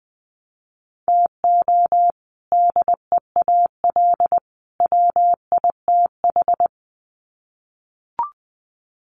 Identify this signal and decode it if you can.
Morse code, heard as a keyed tone, about 20 wpm, 700 Hz. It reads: TO DEAL WITH
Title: Sets of 3 Words 20wpm